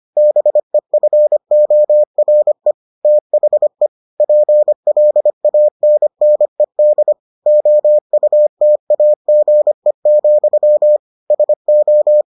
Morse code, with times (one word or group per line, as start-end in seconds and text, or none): 0.2-2.7 BEFORE
3.0-3.9 THE
4.2-7.1 PLANNED
7.5-11.0 OUTAGE,
11.3-12.2 SO